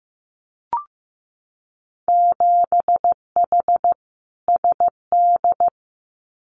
Morse code, heard as a keyed tone, about 15 wpm, 700 Hz. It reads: 7H SD
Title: Winter Field Day Exchanges Encore 15wpm